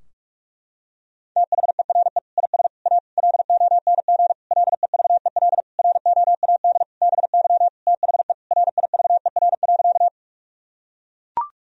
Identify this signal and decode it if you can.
Morse code, heard as a keyed tone, about 41 wpm, 700 Hz. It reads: THERE IS A LONG LEVEL ROAD BY THE RIVER.
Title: Sentences from Top 1000 Words - Review Part 4 - 45wpm